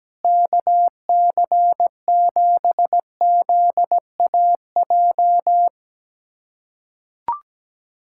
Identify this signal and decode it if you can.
Morse code, heard as a keyed tone, about 17 wpm, 700 Hz. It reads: KC7ZAJ
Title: USA Callsign DE Callsign - 6 Characters - 17wpm